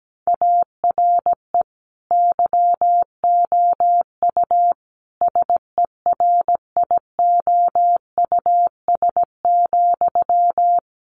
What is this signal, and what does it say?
Morse code, heard as a keyed tone, about 17 wpm, 700 Hz. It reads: ARE YOU SERIOUS,
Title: Sentences from Top 1100 Words 17wpm